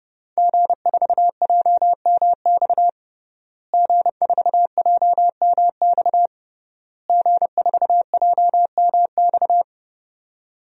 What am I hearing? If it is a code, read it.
Morse code, heard as a keyed tone, about 30 wpm, 700 Hz. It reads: G4JMX G4JMX G4JMX
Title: DXCC-100 Call Signs - 5 Characters - Repeated 3x - 30wpm